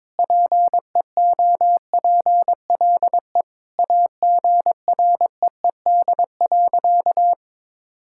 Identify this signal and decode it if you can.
Morse code, heard as a keyed tone, about 22 wpm, 700 Hz. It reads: PEOPLE AGREED.
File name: practice / 22wpm / Sentences with 4-Word Phrases - 22wpm